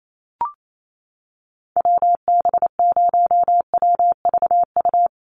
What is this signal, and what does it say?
Morse code, heard as a keyed tone, about 28 wpm, 700 Hz. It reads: WB0WVU
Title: USA Call Signs Repeated 2x 28wpm